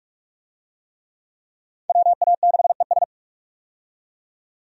Morse code, heard as a keyed tone, about 46 wpm, 700 Hz. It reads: WA6ES
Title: USA Call Signs - 5 Characters - 45wpm